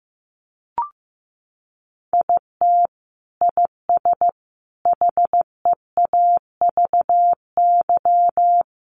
Morse code, heard as a keyed tone, about 15 wpm, 700 Hz. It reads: IT IS HEAVY
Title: Sentences from Top 1100 Words 15wpm